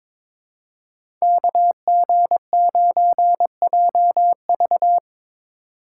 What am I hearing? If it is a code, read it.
Morse code, heard as a keyed tone, about 23 wpm, 700 Hz. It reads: KG9JV
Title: USA Call Signs - 5 Characters - 3x Character Spacing - 22wpm